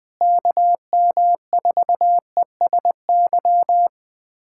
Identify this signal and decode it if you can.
Morse code, heard as a keyed tone, about 20 wpm, 700 Hz. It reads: KM4ESY
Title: USA Callsign DE Callsign 20wpm